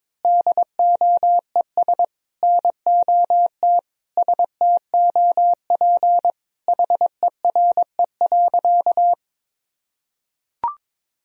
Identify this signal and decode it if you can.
Morse code, heard as a keyed tone, about 22 wpm, 700 Hz. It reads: DOES NOT STOP HERE.
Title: Sentences from Top 1000 Words - Review Part 2 - 22wpm